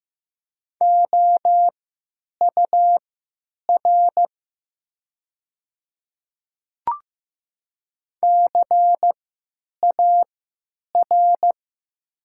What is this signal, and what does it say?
Morse code, heard as a keyed tone, about 15 wpm, spaced as for 9 wpm, 700 Hz. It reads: OUR CAR
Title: 3x Character Spacing - Top 300 Words - 15wpm